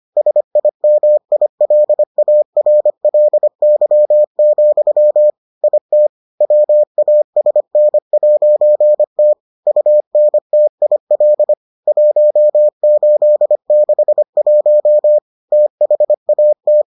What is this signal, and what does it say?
Morse code, heard as a keyed tone, about 25 wpm, 600 Hz. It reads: SIMILARLY, IT WASN'T UNTIL 1861 THAT